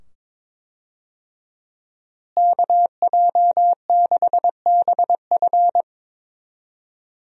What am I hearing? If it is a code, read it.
Morse code, heard as a keyed tone, about 22 wpm, 700 Hz. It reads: KJ6BF
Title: USA Call Signs - 5 Characters - Repeated 2x - 22wpm